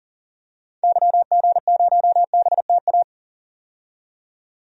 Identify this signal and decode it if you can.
Morse code, heard as a keyed tone, about 40 wpm, 700 Hz. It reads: YG0BTA
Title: DXCC-100 Call Signs - 6 Characters - 40wpm